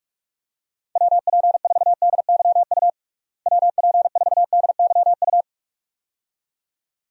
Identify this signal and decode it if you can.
Morse code, heard as a keyed tone, about 41 wpm, 700 Hz. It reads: WP4DYU WP4DYU
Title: USA Call Signs - 6 Characters - Repeated 2x - 45wpm